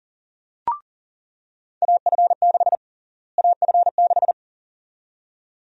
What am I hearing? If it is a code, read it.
Morse code, heard as a keyed tone, about 40 wpm, 700 Hz. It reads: AF6 AF6
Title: USA Call Sign Prefixes - Repeated 2x - 40wpm